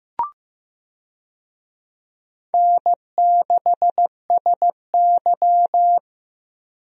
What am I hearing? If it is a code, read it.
Morse code, heard as a keyed tone, about 15 wpm, 700 Hz. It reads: N6SY
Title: USA Call Signs - 4 Characters - Repeated 3x - 15wpm